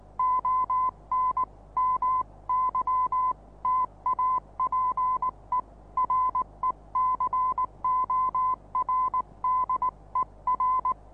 Morse code, heard as a keyed tone, about 19 wpm, 1030 Hz. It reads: ON MY TAPE RECORDER
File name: repos / Morsekurs / DJ4UF Morsen26